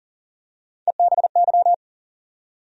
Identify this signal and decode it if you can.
Morse code, heard as a keyed tone, about 38 wpm, 700 Hz. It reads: EBY
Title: USA Call Sign Postfixes - 3 Letters 40wpm